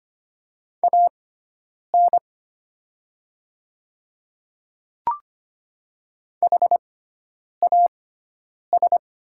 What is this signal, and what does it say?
Morse code, heard as a keyed tone, about 25 wpm, spaced as for 9 wpm, 700 Hz. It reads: AN HAS
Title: 6x Character Spacing - Top 100 Words - 25wpm